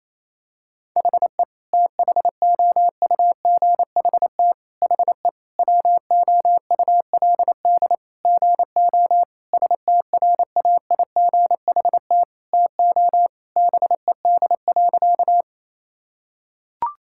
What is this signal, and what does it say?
Morse code, heard as a keyed tone, about 28 wpm, 700 Hz. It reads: HE THOUGHT HE WOULD GO STRAIGHT TO BED.
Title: Sentences from Top 1000 Words - Review Part 6 - 28wpm